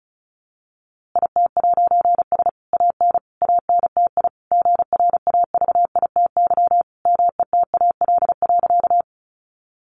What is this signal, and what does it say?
Morse code, heard as a keyed tone, about 34 wpm, 700 Hz. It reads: IT'S AN ANTI GRAVITY METAL.
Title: A Sound Of Thunder 35wpm